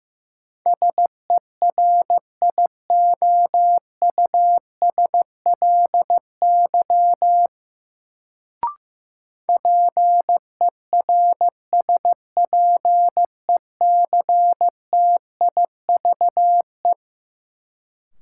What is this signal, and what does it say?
Morse code, heard as a keyed tone, about 15 wpm, 700 Hz. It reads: SERIOUSLY PERSPECTIVE
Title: Sentences from Top 1200 Words - New Word Drill Alt - 15wpm